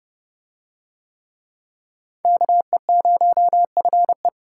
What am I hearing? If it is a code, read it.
Morse code, heard as a keyed tone, about 29 wpm, 700 Hz. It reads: KE0FE